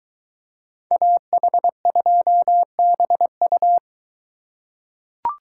Morse code, heard as a keyed tone, about 22 wpm, 700 Hz. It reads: AH2BU